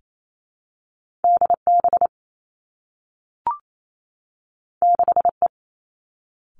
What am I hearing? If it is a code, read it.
Morse code, heard as a keyed tone, about 28 wpm, 700 Hz. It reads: D B 6 E
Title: Two Challenging Letter-Number Combinations 28wpm